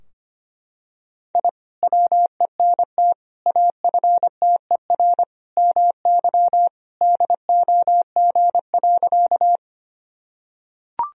Morse code, heard as a keyed tone, about 25 wpm, 700 Hz. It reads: I WENT AFTER MY DOG.
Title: Sentences from Top 1000 Words - Review Part 2 - 25wpm